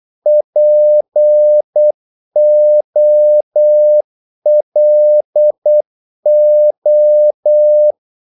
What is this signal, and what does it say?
Morse code, heard as a keyed tone, about 8 wpm, 600 Hz. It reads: POLO
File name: Korte ord DK 8wpm